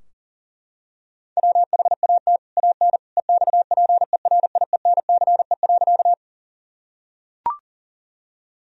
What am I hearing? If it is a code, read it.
Morse code, heard as a keyed tone, about 40 wpm, 700 Hz. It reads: WHAT AN EXPERIENCE.
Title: Sentences from Top 1000 Words - Review Part 3 - 40wpm